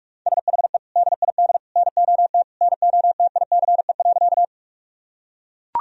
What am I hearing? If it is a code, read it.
Morse code, heard as a keyed tone, about 41 wpm, 700 Hz. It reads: SHE DID NOT NOTICE.
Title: Sentences from Top 1000 Words - Review Part 2 - 45wpm